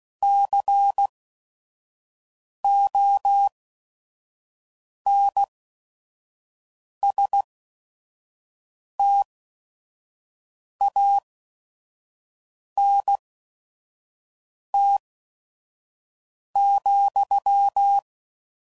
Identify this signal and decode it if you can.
Morse code, heard as a keyed tone, about 16 wpm, spaced as for 5 wpm, 800 Hz. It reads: CONSTANT,
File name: CW Code Groups @5WPM / Technical Subjects / XL